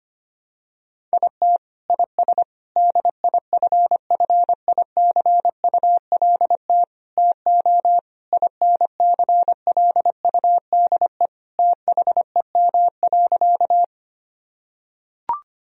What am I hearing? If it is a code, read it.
Morse code, heard as a keyed tone, about 25 wpm, 700 Hz. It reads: IT IS DIFFICULT TO INCLUDE THEM.
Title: Sentences from Top 1000 Words - Review Part 3 - 25wpm